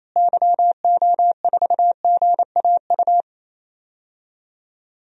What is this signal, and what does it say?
Morse code, heard as a keyed tone, about 28 wpm, 700 Hz. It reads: YO4GAU
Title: DXCC-100 Call Signs - 6 Characters - Repeated 3x - 28wpm